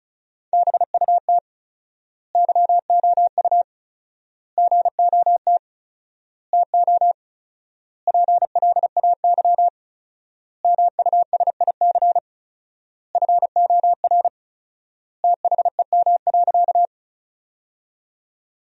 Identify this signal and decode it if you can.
Morse code, heard as a keyed tone, about 34 wpm, 700 Hz. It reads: BUT YOU GOT TO PLAY MUSIC FOR THEM.
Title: 4x Word-Spacing - Sentences from Top 300 Words - 35wpm